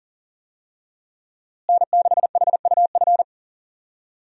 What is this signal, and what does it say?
Morse code, heard as a keyed tone, about 37 wpm, 700 Hz. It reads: N6HUF
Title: USA Call Signs - 5 Characters - Repeated 2x - 40wpm